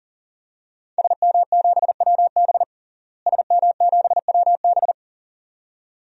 Morse code, heard as a keyed tone, about 37 wpm, 700 Hz. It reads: SM7WB SM7WB
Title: DXCC-100 Call Signs - 5 Characters - Repeated 2x - 40wpm